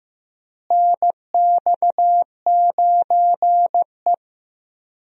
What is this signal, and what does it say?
Morse code, heard as a keyed tone, about 15 wpm, 700 Hz. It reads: NX9E
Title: USA Call Signs - 4 Characters - Repeated 3x - 15wpm